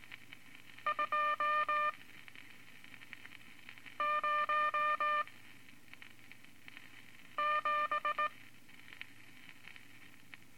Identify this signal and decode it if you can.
Morse code, heard as a keyed tone, about 19 wpm, spaced as for 4 wpm, 1200 Hz. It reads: 207
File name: Morse Code Records / 14B LCM 157 Units V Groups of Five